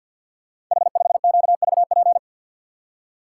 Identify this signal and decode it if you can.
Morse code, heard as a keyed tone, about 50 wpm, 700 Hz. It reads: H5QVP
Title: Code Groups - 5 Letter:Numbers - 50wpm